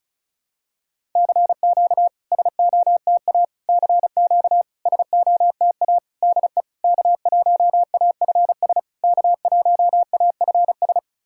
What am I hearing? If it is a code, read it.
Morse code, heard as a keyed tone, about 36 wpm, 690 Hz. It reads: CQ SOTA CQ SOTA DE K1AFS K1AFS
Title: SOTA QSO Practice 35wpm